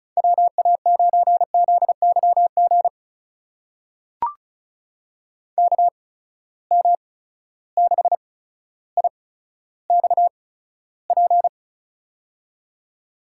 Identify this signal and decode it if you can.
Morse code, heard as a keyed tone, about 36 wpm, spaced as for 15 wpm, 700 Hz. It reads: WA9ZYG KM6IXP
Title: USA Call Signs - 6 Characters - 8x Character Spacing - 35wpm